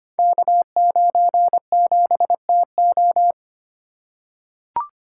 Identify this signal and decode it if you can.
Morse code, heard as a keyed tone, about 25 wpm, 700 Hz. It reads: K97TO
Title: Code Groups - 5 Letter:Numbers - 25wpm